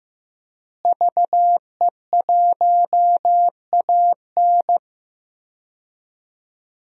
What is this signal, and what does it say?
Morse code, heard as a keyed tone, about 15 wpm, 700 Hz. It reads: VE1AN
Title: DXCC-100 Call Signs - 5 Characters - Repeated 2x - 15wpm